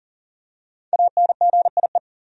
Morse code, heard as a keyed tone, about 40 wpm, 700 Hz. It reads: ANGIE
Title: USA Common Names - 40wpm